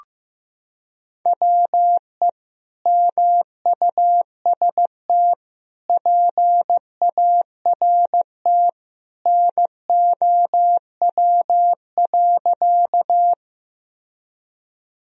Morse code, heard as a keyed tone, about 15 wpm, 700 Hz. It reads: WE MUST PART NOW.
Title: Sentences from Top 1000 Words - Review Part 1 - 15wpm